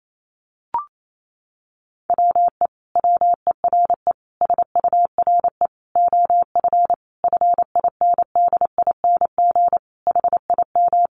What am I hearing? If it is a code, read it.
Morse code, heard as a keyed tone, about 28 wpm, 700 Hz. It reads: WE WERE SURE OF FINDING HIM
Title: Sentences from Top 1100 Words 28wpm